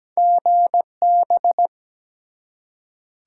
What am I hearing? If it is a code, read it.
Morse code, heard as a keyed tone, about 17 wpm, 700 Hz. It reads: GB
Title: Top 90 QSO Elements 17wpm